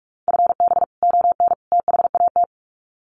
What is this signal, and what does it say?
Morse code, heard as a keyed tone, about 44 wpm, 700 Hz. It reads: FB ON THAT
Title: Common QSO Phrases 45wpm